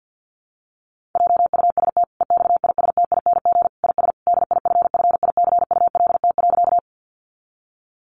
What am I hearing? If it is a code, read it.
Morse code, heard as a keyed tone, about 50 wpm, 700 Hz. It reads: JUST EXISTING IS DIFFICULT.
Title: Sentences from Top 1500 Words 50wpm